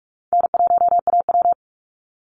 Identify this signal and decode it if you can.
Morse code, heard as a keyed tone, about 44 wpm, 700 Hz. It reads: N1AW
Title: NA SOTA Chaser Callsigns 45wpm